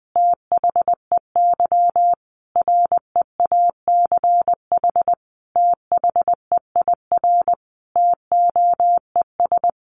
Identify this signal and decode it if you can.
Morse code, heard as a keyed tone, about 20 wpm, 700 Hz. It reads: THEY REACH THEIR TOES
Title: Easy Phrases 20wpm